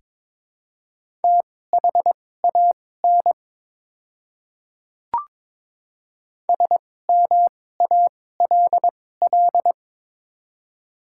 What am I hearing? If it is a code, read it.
Morse code, heard as a keyed tone, about 23 wpm, spaced as for 16 wpm, 700 Hz. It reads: THAN SMALL
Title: 2x Character Spacing - Top 500 Words - 22wpm